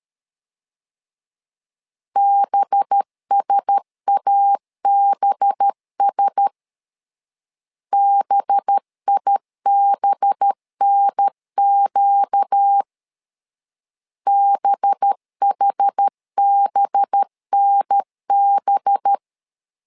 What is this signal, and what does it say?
Morse code, heard as a keyed tone, about 13 wpm, 800 Hz. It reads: BSABS BIBNQ BHBNB
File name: LEZIONI / 5 LBVC / 5 LB 12wpm 1906